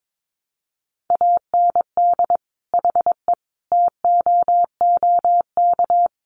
Morse code, heard as a keyed tone, about 22 wpm, 700 Hz. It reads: AND HE TOOK
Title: Sentences from Top 1500 Words 22wpm